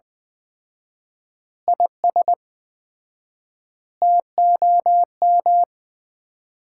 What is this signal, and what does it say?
Morse code, heard as a keyed tone, about 20 wpm, 700 Hz. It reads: IS TOM
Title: Common QSO Phrases - 4x Word-Spacing - 20wpm